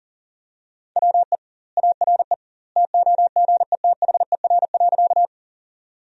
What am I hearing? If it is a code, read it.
Morse code, heard as a keyed tone, about 40 wpm, 700 Hz. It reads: WE ARE TOGETHER.
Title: 2x Word-Spacing - Sentences from Top 300 Words - 40wpm